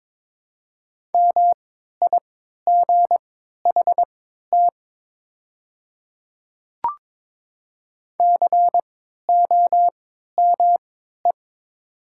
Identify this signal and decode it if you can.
Morse code, heard as a keyed tone, about 21 wpm, spaced as for 12 wpm, 700 Hz. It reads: MIGHT COME
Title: 3x Character Spacing - Top 500 Words - 22wpm